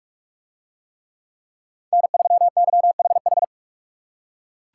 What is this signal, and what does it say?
Morse code, heard as a keyed tone, about 43 wpm, 700 Hz. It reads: N3YHH